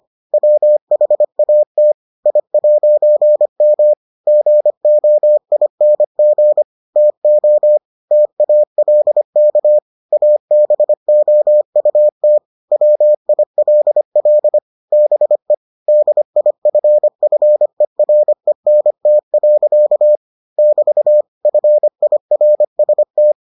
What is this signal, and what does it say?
Morse code, heard as a keyed tone, about 25 wpm, 600 Hz. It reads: WHAT I'M GOING TO TALK ABOUT WILL BE DIFFERENT. <BT> FIRST